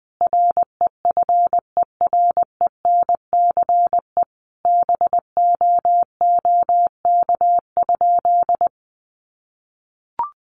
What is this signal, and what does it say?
Morse code, heard as a keyed tone, about 20 wpm, 700 Hz. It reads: REFERENCE BOOK?
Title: Sentences from Top 1200 Words 20wpm